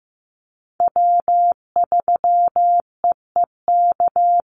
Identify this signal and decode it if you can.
Morse code, heard as a keyed tone, about 15 wpm, 700 Hz. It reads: W3EEK
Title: Call Signs Repeated 3x - 15wpm